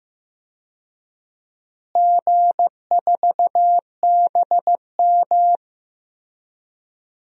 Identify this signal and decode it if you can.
Morse code, heard as a keyed tone, about 15 wpm, 700 Hz. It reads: G4BM
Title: DXCC-100 Call Signs - 4 Characters - Repeated 2x - 15wpm